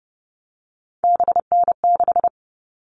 Morse code, heard as a keyed tone, about 30 wpm, 700 Hz. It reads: BN6
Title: All Cut Numbers - 3-Digit 30wpm